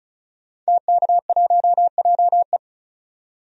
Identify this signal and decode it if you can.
Morse code, heard as a keyed tone, about 34 wpm, 700 Hz. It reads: TK1JE
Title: DXCC-100 Call Signs - 5 Characters - 35wpm